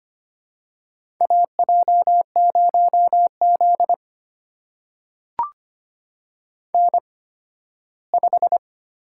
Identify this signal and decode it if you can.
Morse code, heard as a keyed tone, about 25 wpm, spaced as for 15 wpm, 700 Hz. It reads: AJ0Z N5
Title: USA Call Signs - 4 Characters - 8x Character Spacing - 25wpm